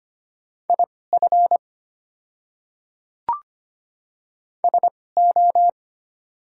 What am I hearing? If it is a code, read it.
Morse code, heard as a keyed tone, about 25 wpm, spaced as for 18 wpm, 690 Hz. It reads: IF SO
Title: 2x Character Spacing - Top 200 Words - 25wpm